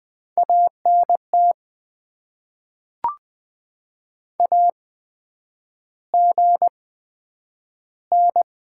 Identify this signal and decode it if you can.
Morse code, heard as a keyed tone, about 20 wpm, spaced as for 9 wpm, 700 Hz. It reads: ANT AGN
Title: Top 20 QSO Elements - 8x Character Spacing - 20wpm